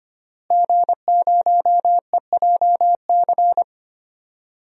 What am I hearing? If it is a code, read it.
Morse code, heard as a keyed tone, about 25 wpm, 700 Hz. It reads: G0EJC
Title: DXCC-100 Call Signs - 5 Characters - Repeated 3x - 25wpm